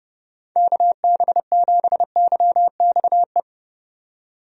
Morse code, heard as a keyed tone, about 32 wpm, 700 Hz. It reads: KB7YXE
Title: USA Call Signs - 6 Characters - 30wpm